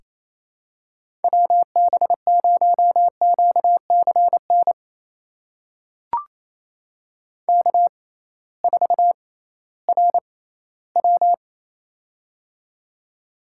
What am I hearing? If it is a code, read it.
Morse code, heard as a keyed tone, about 28 wpm, spaced as for 16 wpm, 700 Hz. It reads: WB0QCN K4RW
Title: USA Call Signs - 6x Character Spacing - 28wpm